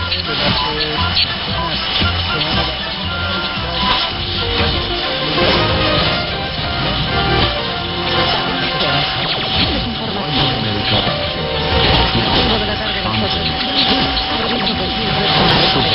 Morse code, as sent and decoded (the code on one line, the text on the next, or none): none